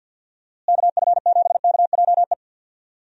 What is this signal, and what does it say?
Morse code, heard as a keyed tone, about 51 wpm, 700 Hz. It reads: KF7KJE
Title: USA Call Signs - 6 Characters - Repeated 2x - 50wpm